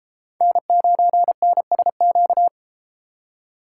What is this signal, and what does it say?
Morse code, heard as a keyed tone, about 30 wpm, 700 Hz. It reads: N9NSQ